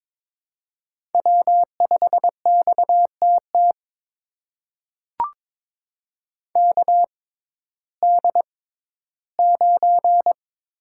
Morse code, heard as a keyed tone, about 22 wpm, spaced as for 13 wpm, 700 Hz. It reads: W5XTT KD9
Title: USA Call Signs - 6x Character Spacing - 22wpm